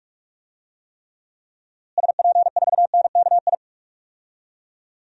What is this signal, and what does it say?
Morse code, heard as a keyed tone, about 44 wpm, 700 Hz. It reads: SP4NKI
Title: DXCC-100 Call Signs - 6 Characters - Repeated 2x - 45wpm